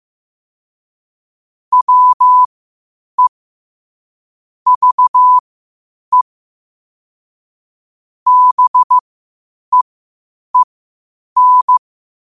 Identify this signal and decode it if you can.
Morse code, heard as a keyed tone, about 15 wpm, spaced as for 8 wpm, 1000 Hz. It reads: WE VE BEEN
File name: qotdfiles / 8wpm